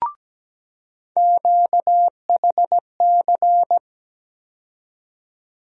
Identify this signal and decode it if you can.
Morse code, heard as a keyed tone, about 17 wpm, 700 Hz. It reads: QHC
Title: USA Call Sign Postfixes - 3 Letters 17wpm